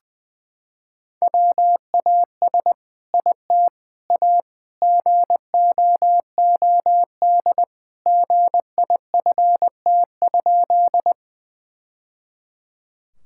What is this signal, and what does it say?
Morse code, heard as a keyed tone, about 20 wpm, 700 Hz. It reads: WAS IT A GOOD GIFT?
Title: Sentences from Top 1000 Words - Review Part 6 - 20wpm